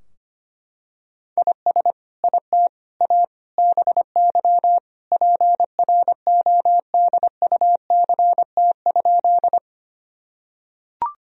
Morse code, heard as a keyed tone, about 25 wpm, 700 Hz. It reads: IS IT A BY PRODUCT?
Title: Sentences from Top 1000 Words - Review Part 2 - 25wpm